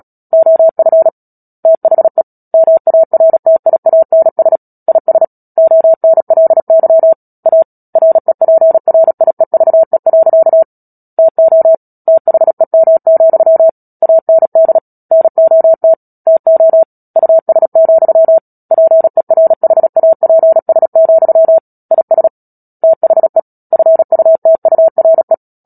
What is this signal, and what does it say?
Morse code, heard as a keyed tone, about 37 wpm, 650 Hz. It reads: OF THE MARTIANS IS ONLY A REPRIEVE. TO THEM, AND NOT TO US, PERHAPS, IS THE FUTURE
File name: Book Two / 10-Chapter 10 - Epilogue - 36 WPM